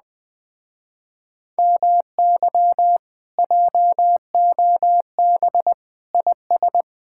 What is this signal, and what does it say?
Morse code, heard as a keyed tone, about 20 wpm, 700 Hz. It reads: MY JOB IS
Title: Speed-Racing Common QSO Phrases Encore - 17wpm